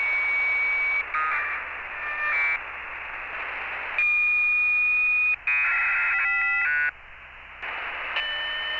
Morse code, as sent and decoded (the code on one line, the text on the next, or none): none